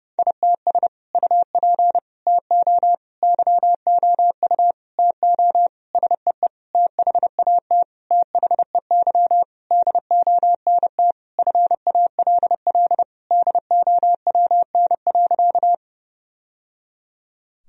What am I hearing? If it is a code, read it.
Morse code, heard as a keyed tone, about 32 wpm, 700 Hz. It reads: ITS UP TO YOU TO SEE THAT THEY DONT FALL DOWN.
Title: Sentences from Top 1000 Words - Review Part 2 - 30wpm